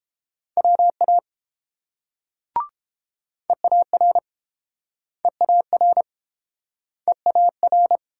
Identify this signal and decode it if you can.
Morse code, heard as a keyed tone, about 29 wpm, 700 Hz. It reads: WA EAR EAR EAR